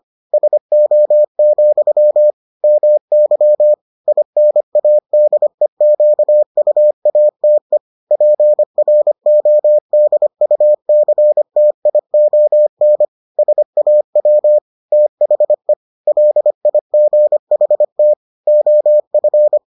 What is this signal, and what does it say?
Morse code, heard as a keyed tone, about 25 wpm, 600 Hz. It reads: SO, MY INADEQUATE PRODUCTION SAW THE LIGHT OF